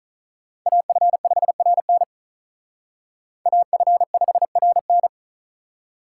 Morse code, heard as a keyed tone, about 45 wpm, 700 Hz. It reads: AF5RN AF5RN